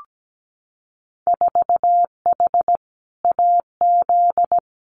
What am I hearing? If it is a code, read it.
Morse code, heard as a keyed tone, about 17 wpm, 700 Hz. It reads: 4H AZ
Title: Winter Field Day Exchanges Encore 17wpm